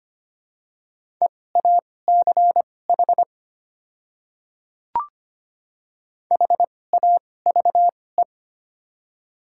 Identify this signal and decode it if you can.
Morse code, heard as a keyed tone, about 25 wpm, spaced as for 18 wpm, 700 Hz. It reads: EACH HAVE
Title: 2x Character Spacing - Top 200 Words - 25wpm